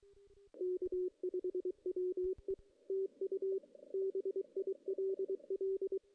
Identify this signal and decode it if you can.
Morse code, heard as a keyed tone, about 22 wpm, 410 Hz, with radio noise. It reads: K5WE TU BILL
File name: 2016 JAN CW NAQP / 2016 Jan naqp k5ot - 0-5